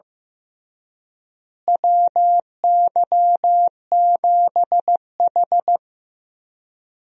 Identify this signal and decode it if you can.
Morse code, heard as a keyed tone, about 15 wpm, 700 Hz. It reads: WY7H
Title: USA Call Signs - 4 Characters - Repeated 3x - 15wpm